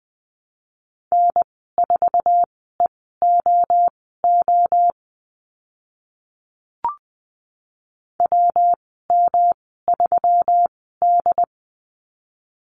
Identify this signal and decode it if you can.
Morse code, heard as a keyed tone, about 20 wpm, spaced as for 15 wpm, 700 Hz. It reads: N4EOO WM3D
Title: 122 Lesson 20g - Number 3 - Call Signs 20wpm